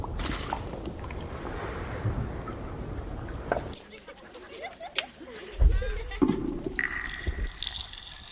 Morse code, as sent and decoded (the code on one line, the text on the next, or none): none